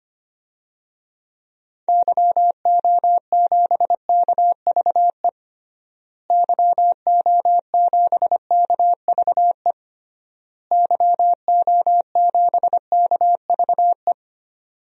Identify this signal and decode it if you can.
Morse code, heard as a keyed tone, about 25 wpm, 700 Hz. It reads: YO7KVE YO7KVE YO7KVE
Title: DXCC-100 Call Signs - 6 Characters - Repeated 3x - 25wpm